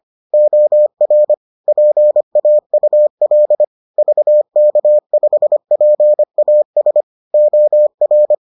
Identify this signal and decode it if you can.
Morse code, heard as a keyed tone, about 25 wpm, 600 Hz. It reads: OR PAUL VK5PAS OR